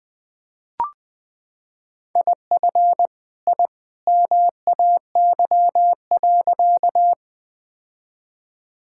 Morse code, heard as a keyed tone, about 20 wpm, 700 Hz. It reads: IF I MAY.
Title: Sentences with 2-Word Phrases - 20wpm